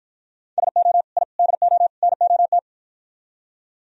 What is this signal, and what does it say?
Morse code, heard as a keyed tone, about 58 wpm, 700 Hz. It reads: SO I DO NOT